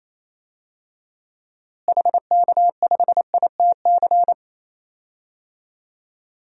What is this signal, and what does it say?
Morse code, heard as a keyed tone, about 28 wpm, 700 Hz. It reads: HK5ITC
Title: DXCC-100 Call Signs - 6 Characters - Repeated 2x - 28wpm